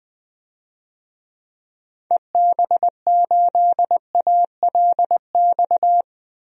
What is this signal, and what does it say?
Morse code, heard as a keyed tone, about 20 wpm, 700 Hz. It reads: EB8ALX
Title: DXCC-100 Call Signs - 6 Characters - 20wpm